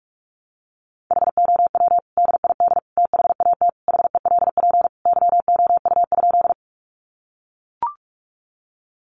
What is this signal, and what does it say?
Morse code, heard as a keyed tone, about 44 wpm, 700 Hz. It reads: HOW DID THAT HELP YOU?
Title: Sentences from Top 200 Words 45wpm